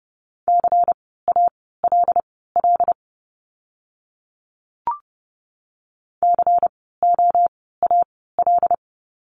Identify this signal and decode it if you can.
Morse code, heard as a keyed tone, about 30 wpm, spaced as for 17 wpm, 700 Hz. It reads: CALL COAL
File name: Lesson 16f - Letter c - Words 30wpm